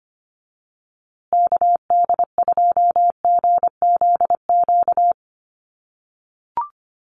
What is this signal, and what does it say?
Morse code, heard as a keyed tone, about 25 wpm, 700 Hz. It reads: KD2GZQ
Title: USA Call Signs Repeated 3x 25wpm